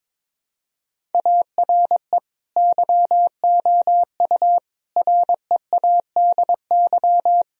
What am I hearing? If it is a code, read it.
Morse code, heard as a keyed tone, about 22 wpm, 700 Hz. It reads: ARE YOU READY
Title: Sentences from Top 1000 Words - Review Part 1 - 22wpm